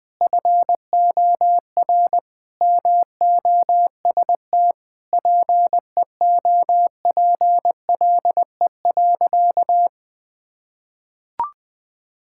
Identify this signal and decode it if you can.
Morse code, heard as a keyed tone, about 20 wpm, 700 Hz. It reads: FOR MOST PEOPLE.
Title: Sentences with 6-Word Phrases - 20wpm